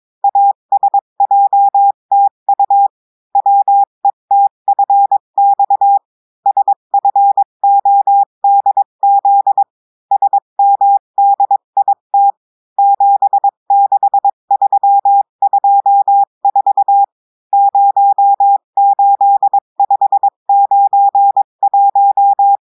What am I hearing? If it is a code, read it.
Morse code, heard as a keyed tone, about 22 wpm, 800 Hz. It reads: ASJTU WETFX SFODZ SMDIT 76324 08591
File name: cours-du-mercredi / audios / cours-853-du-2-mai-2024-etxte chiffre et texte clair-a-20-wpm